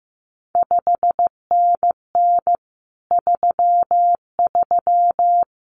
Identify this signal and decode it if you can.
Morse code, heard as a keyed tone, about 15 wpm, 700 Hz. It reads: 5NN 33
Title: CQ WW DX Contest Exchange Encore - 15wpm